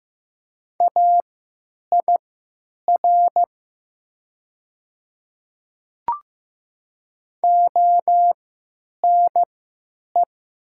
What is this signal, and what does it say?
Morse code, heard as a keyed tone, about 15 wpm, spaced as for 9 wpm, 700 Hz. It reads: AIR ONE
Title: 3x Character Spacing - Top 200 Words - 15wpm